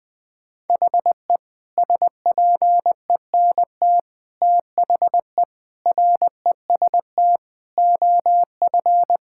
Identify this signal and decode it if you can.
Morse code, heard as a keyed tone, about 20 wpm, 700 Hz. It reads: HE SPENT THE REST OF
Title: Sentences with 5-Word Phrases - 20wpm